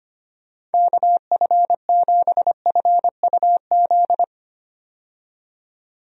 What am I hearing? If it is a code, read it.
Morse code, heard as a keyed tone, about 25 wpm, 700 Hz. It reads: KF7FUZ
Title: USA Call Signs - 6 Characters - 25wpm